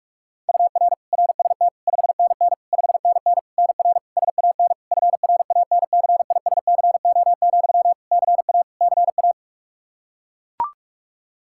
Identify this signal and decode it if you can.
Morse code, heard as a keyed tone, about 41 wpm, 700 Hz. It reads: UR RST 5NN 5NN NR SAN FRANCISCO, CA CA
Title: Practice QSO - Exchange 1 - Beginning - 45wpm